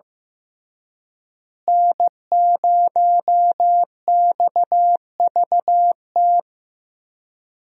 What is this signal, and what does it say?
Morse code, heard as a keyed tone, about 15 wpm, 700 Hz. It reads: N0XVT
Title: USA Call Signs - 5 Characters - Repeated 2x - 15wpm